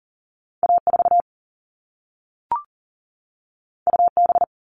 Morse code, heard as a keyed tone, about 40 wpm, 700 Hz. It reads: A4 UB
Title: All Cut Numbers - 2-Digit 40wpm